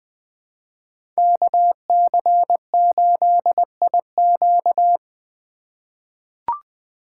Morse code, heard as a keyed tone, about 20 wpm, 700 Hz. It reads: KC8IQ
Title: 2023 POTA Hunter Call Signs - 20wpm